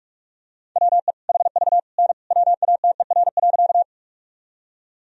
Morse code, heard as a keyed tone, about 46 wpm, 700 Hz. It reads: WE HV N WATER.
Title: Sentences from Top 100 Words and Abbreviations 45wpm